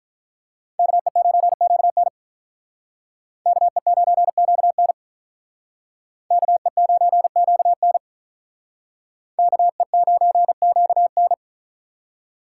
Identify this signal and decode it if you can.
Morse code, heard as a keyed tone, about 45 wpm, 700 Hz. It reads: KE9QN KE9QN KE9QN KE9QN